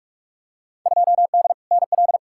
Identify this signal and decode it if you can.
Morse code, heard as a keyed tone, about 45 wpm, 700 Hz. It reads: 2D NL
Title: Field Day Exchanges Encore - 45wpm